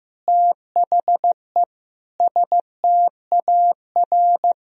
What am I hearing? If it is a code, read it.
Morse code, heard as a keyed tone, about 15 wpm, 700 Hz. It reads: THE STAR
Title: Sentences from Top 1000 Words - Review Part 2 - 15wpm